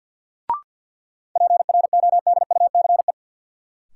Morse code, heard as a keyed tone, about 50 wpm, 700 Hz. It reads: PRODUCE
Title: Sentences from Top 1000 Words - Review Part 3 - New Word Drill - 50wpm